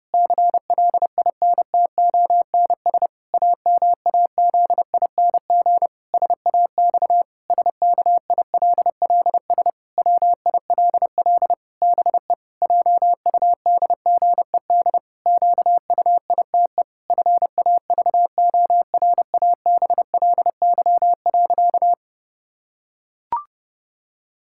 Morse code, heard as a keyed tone, about 32 wpm, 700 Hz. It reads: CLINTONS AMAZING SAX SKILLS WILL BE JUDGED QUITE FAVORABLY.
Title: Pangram Challenge v2 - 30wpm